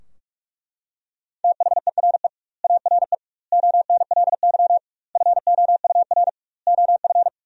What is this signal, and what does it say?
Morse code, heard as a keyed tone, about 41 wpm, 700 Hz. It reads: THERE ARE ONLY FOUR OF
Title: 2x Word-Spacing - Sentences from Top 300 Words - 45wpm